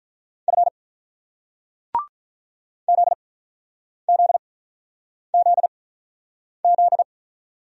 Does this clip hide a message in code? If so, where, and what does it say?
Yes: Morse code from 0.5 to 7.0 s, heard as a keyed tone, about 47 wpm, spaced as for 9 wpm, 690 Hz. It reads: F ZZZZ